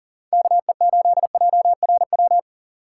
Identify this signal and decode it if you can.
Morse code, heard as a keyed tone, about 40 wpm, 700 Hz. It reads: KE8JRW
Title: USA Call Signs - 2x Character Spacing - 40wpm